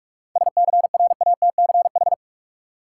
Morse code, heard as a keyed tone, about 46 wpm, 700 Hz. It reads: SCRATCH
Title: Seven Letter Words - 45wpm